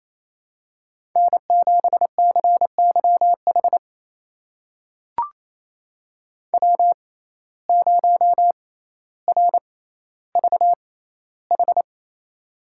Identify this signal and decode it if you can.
Morse code, heard as a keyed tone, about 28 wpm, spaced as for 14 wpm, 700 Hz. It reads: N7CYH W0RVH
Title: USA Call Signs - 5 Characters - 6x Character Spacing - 28wpm